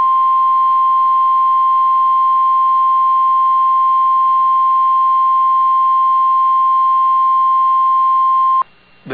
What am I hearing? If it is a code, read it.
Emergency Alert System (SAME) tones (the 1050 Hz attention tone). It is heard from 0.4 to 8.2 s.